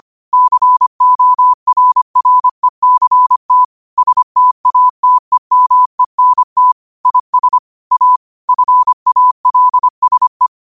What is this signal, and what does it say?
Morse code, heard as a keyed tone, about 25 wpm, 1000 Hz. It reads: CORRECT STATEMENT IS A FALSE
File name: qotdfiles / 25wpm